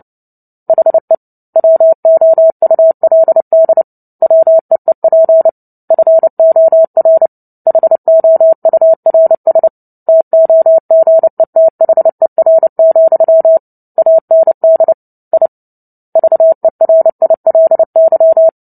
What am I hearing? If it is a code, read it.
Morse code, heard as a keyed tone, about 30 wpm, 650 Hz. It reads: HE WOULD WEEP FOR HOURS TOGETHER, AND I VERILY
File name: Book Two / 03-Chapter 3 - 29 WPM